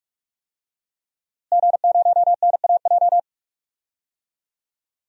Morse code, heard as a keyed tone, about 42 wpm, 700 Hz. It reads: G0NAJ
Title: DXCC-100 Call Signs - 5 Characters - Repeated 2x - 45wpm